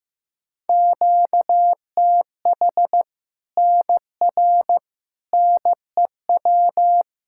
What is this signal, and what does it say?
Morse code, heard as a keyed tone, about 15 wpm, 700 Hz. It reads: QTH NR NEW
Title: Practice QSO - Exchange 1 - Beginning - 15wpm